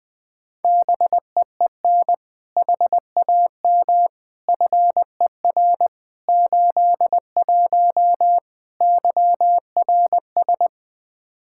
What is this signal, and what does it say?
Morse code, heard as a keyed tone, about 20 wpm, 700 Hz. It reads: BEEN HAM FER 81 YRS
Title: Practice QSO - Exchange 3 - More Info - 20wpm